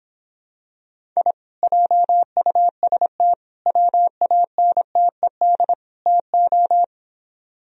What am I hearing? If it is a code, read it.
Morse code, heard as a keyed tone, about 26 wpm, 700 Hz. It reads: I JUST WANTED TO